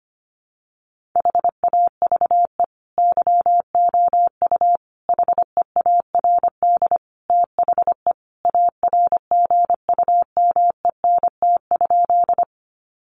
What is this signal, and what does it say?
Morse code, heard as a keyed tone, about 25 wpm, 700 Hz. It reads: HAVE YOU HEARD THE ARGUMENT?
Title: Sentences from Top 1400 Words 25wpm